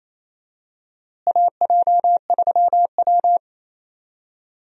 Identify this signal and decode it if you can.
Morse code, heard as a keyed tone, about 28 wpm, 700 Hz. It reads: AJ3W
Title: USA Call Signs - 4 Characters - Repeated 2x - 28wpm